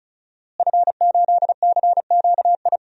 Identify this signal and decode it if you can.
Morse code, heard as a keyed tone, about 34 wpm, 700 Hz. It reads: F8CQI
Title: DXCC-100 Call Signs - 5 Characters - Repeated 3x - 35wpm